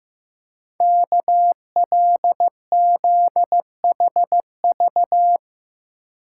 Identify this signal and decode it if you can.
Morse code, heard as a keyed tone, about 15 wpm, 700 Hz. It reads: KLZHV
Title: Code Groups - 5 Letters - 15wpm